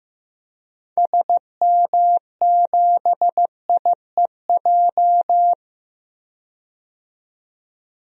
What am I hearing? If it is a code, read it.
Morse code, heard as a keyed tone, about 15 wpm, 700 Hz. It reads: SM7IEJ
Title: DXCC-100 Call Signs - 6 Characters - 15wpm